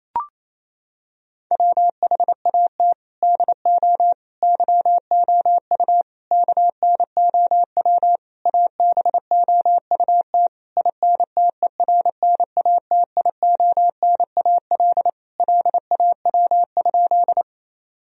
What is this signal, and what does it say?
Morse code, heard as a keyed tone, about 27 wpm, 700 Hz. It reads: WHAT DO YOU KNOW ABOUT INTERNATIONAL LAW?
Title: Sentences from Top 1000 Words - Review Part 4 - 28wpm